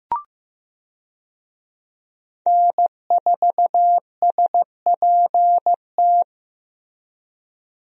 Morse code, heard as a keyed tone, about 15 wpm, 700 Hz. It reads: N4SPT
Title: USA Call Signs - 5 Characters - Repeated 2x - 15wpm